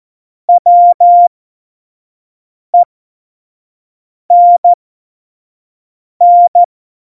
Morse code, heard as a keyed tone, about 14 wpm, spaced as for 5 wpm, 700 Hz. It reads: WENN
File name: Morsekurs / Witze / Witz 08. Januar 2019, 25 F70